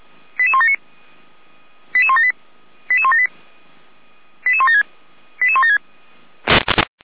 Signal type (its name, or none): five-tone selective call